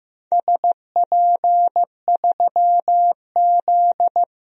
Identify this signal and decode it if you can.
Morse code, heard as a keyed tone, about 15 wpm, 700 Hz. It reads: SP3Z
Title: DXCC-100 Call Signs - 4 Characters - 15wpm